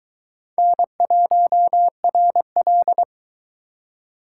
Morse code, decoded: N1RL

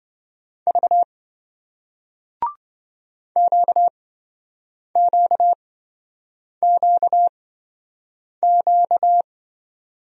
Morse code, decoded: V QQQQ

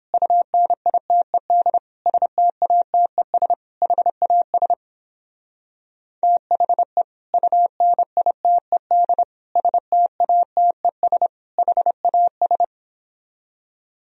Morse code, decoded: UNITED STATES HAS THE UNITED STATES HAS

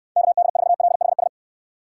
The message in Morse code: KD4LHS

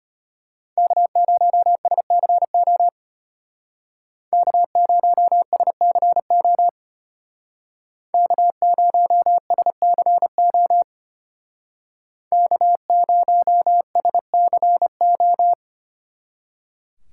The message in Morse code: K0SCO K0SCO K0SCO K0SCO

